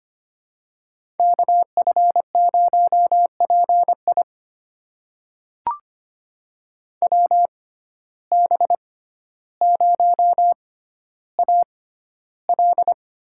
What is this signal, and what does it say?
Morse code, heard as a keyed tone, about 25 wpm, spaced as for 13 wpm, 700 Hz. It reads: KF0PI WB0AL
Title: USA Call Signs - 5 Characters - 6x Character Spacing - 25wpm